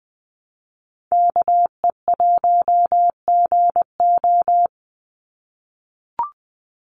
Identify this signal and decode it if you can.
Morse code, heard as a keyed tone, about 20 wpm, 700 Hz. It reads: KE1GO
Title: USA Call Signs 20wpm